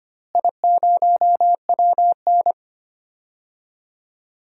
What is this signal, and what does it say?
Morse code, heard as a keyed tone, about 25 wpm, 700 Hz. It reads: I0WN
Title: DXCC-100 Call Signs - 4 Characters - Repeated 3x - 25wpm